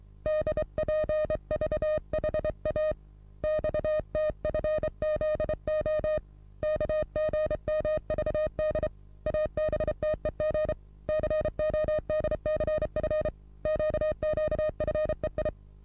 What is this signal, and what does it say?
Morse code, heard as a keyed tone, about 24 wpm, 620 Hz. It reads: DPVHA XTFZO KGMVD ABTEG CODCF QQFEI
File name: repos / Morsekurs / DJ4UF Morsen27